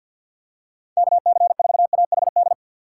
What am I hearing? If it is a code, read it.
Morse code, heard as a keyed tone, about 52 wpm, 700 Hz. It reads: KC4AHD
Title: USA Call Signs - 6x Character Spacing - 50wpm